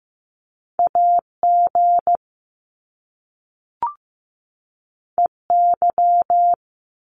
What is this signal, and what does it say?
Morse code, heard as a keyed tone, about 15 wpm, 700 Hz. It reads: AG EY
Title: Top Two Letter Combinations in Words 15wpm